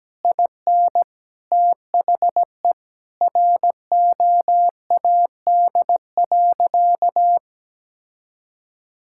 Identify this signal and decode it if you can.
Morse code, heard as a keyed tone, about 17 wpm, 700 Hz. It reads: IN THE ROAD.
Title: Sentences from Top 1600 Words 17wpm